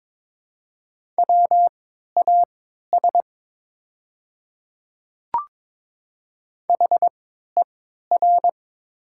Morse code, heard as a keyed tone, about 23 wpm, spaced as for 13 wpm, 700 Hz. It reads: WAS HER
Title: 3x Character Spacing - Top 100 Words - 22wpm